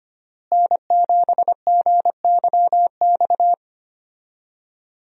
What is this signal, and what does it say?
Morse code, heard as a keyed tone, about 25 wpm, 700 Hz. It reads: N7GYX
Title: USA Call Signs - 5 Characters - Repeated 3x - 25wpm